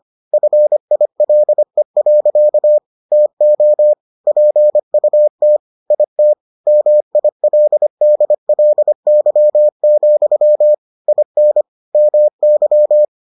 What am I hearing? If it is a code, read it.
Morse code, heard as a keyed tone, about 25 wpm, 600 Hz. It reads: FILE. TO PUT IT MILDLY, IN MY